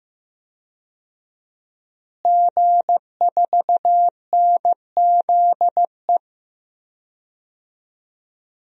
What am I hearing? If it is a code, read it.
Morse code, heard as a keyed tone, about 15 wpm, 700 Hz. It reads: G4NZE
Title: DXCC-100 Call Signs - 5 Characters - Repeated 2x - 15wpm